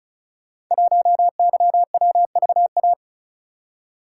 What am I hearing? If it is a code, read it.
Morse code, heard as a keyed tone, about 36 wpm, 700 Hz. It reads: 1YWVA
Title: Code Groups - 5 Letter:Numbers - 35wpm